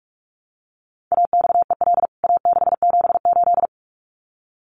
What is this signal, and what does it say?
Morse code, heard as a keyed tone, about 44 wpm, 700 Hz. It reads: AXEL A678
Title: CWT Exchange 45wpm